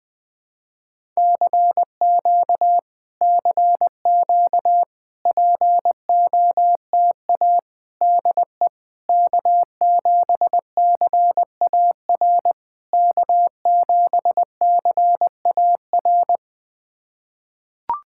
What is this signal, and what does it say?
Morse code, heard as a keyed tone, about 20 wpm, 690 Hz. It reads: CQ CQ POTA DE K7CAR K7CAR
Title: POTA QSO Practice - 20wpm